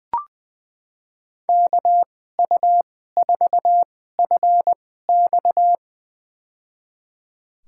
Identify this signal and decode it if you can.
Morse code, heard as a keyed tone, about 20 wpm, spaced as for 14 wpm, 700 Hz. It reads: KU4FX
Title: USA Call Signs - 5 Characters - 2x Character Spacing - 20wpm